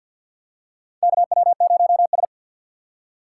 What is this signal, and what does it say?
Morse code, heard as a keyed tone, about 51 wpm, 690 Hz. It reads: KW0S
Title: USA Call Signs - 4x Character Spacing - 50wpm